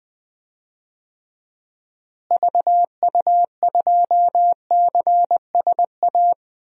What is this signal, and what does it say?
Morse code, heard as a keyed tone, about 20 wpm, 700 Hz. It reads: VU2CSA
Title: DXCC-100 Call Signs - 6 Characters - Repeated 2x - 20wpm